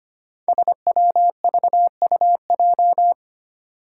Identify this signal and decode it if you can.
Morse code, heard as a keyed tone, about 25 wpm, 700 Hz. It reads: SWVUJ
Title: Five Challenging Letter-Number Combinations 25wpm